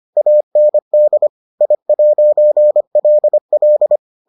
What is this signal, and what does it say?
Morse code, heard as a keyed tone, about 25 wpm, 600 Hz. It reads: AND I'LL